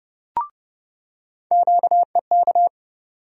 Morse code, heard as a keyed tone, about 30 wpm, 700 Hz. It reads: QEK
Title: USA Call Sign Postfixes - 3 Letters 30wpm